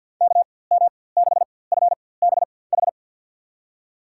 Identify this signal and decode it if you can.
Morse code, heard as a keyed tone, about 53 wpm, spaced as for 23 wpm, 700 Hz. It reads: KM6FBH
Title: USA Call Signs - 4x Character Spacing - 50wpm